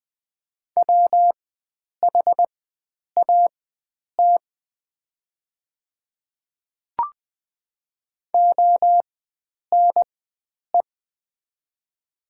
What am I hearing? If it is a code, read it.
Morse code, heard as a keyed tone, about 20 wpm, spaced as for 9 wpm, 700 Hz. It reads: WHAT ONE
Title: 4x Character Spacing - Top 100 Words - 20wpm